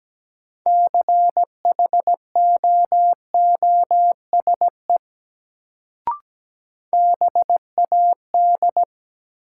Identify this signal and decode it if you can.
Morse code, heard as a keyed tone, about 17 wpm, 700 Hz. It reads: CHOOSE BAD
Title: Sentences from Top 700 Words - New Word Drill - 17wpm